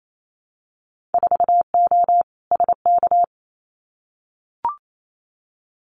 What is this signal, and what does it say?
Morse code, heard as a keyed tone, about 28 wpm, 700 Hz. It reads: 4O SK
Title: Winter Field Day Exchanges Encore 28wpm